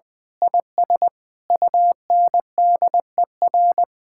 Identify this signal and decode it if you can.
Morse code, heard as a keyed tone, about 20 wpm, 700 Hz. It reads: IS UNDER
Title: Sentences with 9-Word Phrases - Spoken Beginning - 20wpm